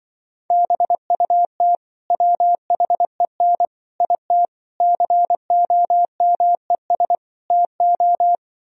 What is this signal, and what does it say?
Morse code, heard as a keyed tone, about 25 wpm, 700 Hz. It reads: BUT WHEN IT COMES TO
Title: Speed-Racing - Sets of 5 Words 20wpm